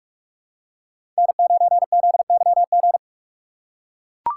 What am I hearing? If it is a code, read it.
Morse code, heard as a keyed tone, about 41 wpm, 700 Hz. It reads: N9ZYG
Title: USA Call Signs - 6x Character Spacing - 45wpm